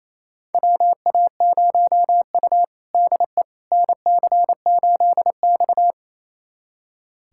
Morse code, heard as a keyed tone, about 28 wpm, 700 Hz. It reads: WA0U DE NC8X
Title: USA Callsign DE Callsign - 4 Characters - 28wpm